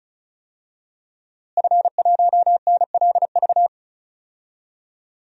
Morse code, heard as a keyed tone, about 34 wpm, 700 Hz. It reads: F1NLV
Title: DXCC-100 Call Signs - 5 Characters - Repeated 3x - 35wpm